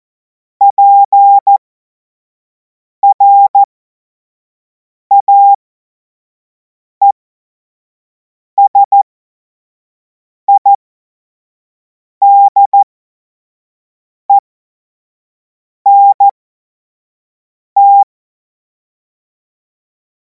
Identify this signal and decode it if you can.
Morse code, heard as a keyed tone, about 14 wpm, spaced as for 5 wpm, 800 Hz. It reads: PRAESIDENT
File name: Morsekurs / Witze / Witz 12. Mai 2026 25 F70